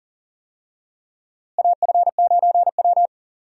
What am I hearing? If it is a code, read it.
Morse code, heard as a keyed tone, about 40 wpm, 700 Hz. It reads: AF9W
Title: USA Call Signs - 4 Characters - Repeated 3x - 40wpm